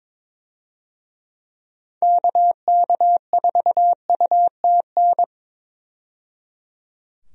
Morse code, heard as a keyed tone, about 22 wpm, 700 Hz. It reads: KK4UTN
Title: USA Call Signs - 6 Characters - 22wpm